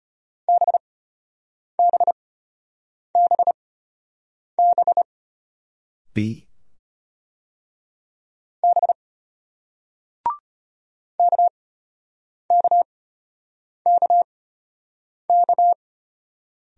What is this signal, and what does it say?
Morse code, heard as a keyed tone, about 32 wpm, spaced as for 8 wpm, 700 Hz. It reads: BBBB B KKKK